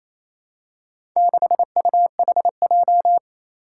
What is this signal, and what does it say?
Morse code, heard as a keyed tone, about 29 wpm, 700 Hz. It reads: 6UHJ
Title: Four Challenging Letter-Number Combinations 28wpm